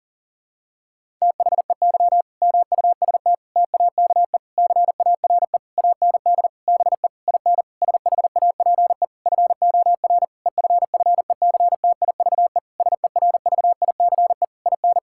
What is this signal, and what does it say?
Morse code, heard as a keyed tone, about 40 wpm, 700 Hz. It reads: THEY MUST TAKE CARE AND BE IN SHAPE FOR EFFECTIVE SERVICE IN
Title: Sentences from Top 1000 Words - Review Part 6 - 40wpm